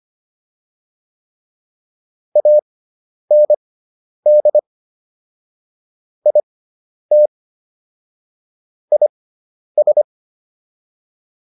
Morse code, heard as a keyed tone, about 25 wpm, spaced as for 10 wpm, 600 Hz. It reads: AND IT IS